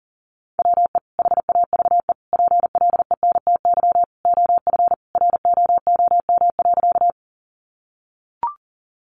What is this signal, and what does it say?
Morse code, heard as a keyed tone, about 40 wpm, 700 Hz. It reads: WE HAVE PLENTY OF ROOM.
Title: Sentences from Top 1300 Words 40wpm